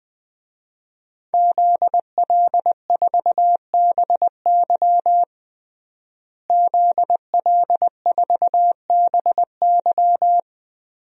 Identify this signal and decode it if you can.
Morse code, heard as a keyed tone, about 20 wpm, 700 Hz. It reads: ZL4BY ZL4BY
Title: DXCC-100 Call Signs - 5 Characters - Repeated 2x - 20wpm